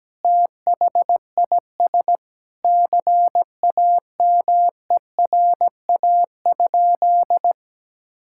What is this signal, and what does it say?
Morse code, heard as a keyed tone, about 17 wpm, 700 Hz. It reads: THIS CAMERA?
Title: Sentences from Top 1000 Words - Review Part 5 - 17wpm